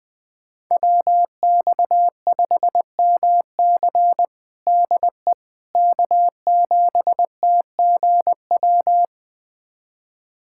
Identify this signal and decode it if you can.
Morse code, heard as a keyed tone, about 20 wpm, 700 Hz. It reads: WX5MC DE K7TGW
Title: USA Callsign DE Callsign - 5 Characters - 20wpm